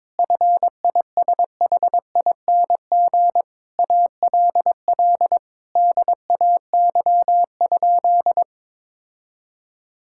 Morse code, heard as a keyed tone, about 21 wpm, 700 Hz. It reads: FISHING ALL DAY?
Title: Sentences from Top 1000 Words - Review Part 5 - 22wpm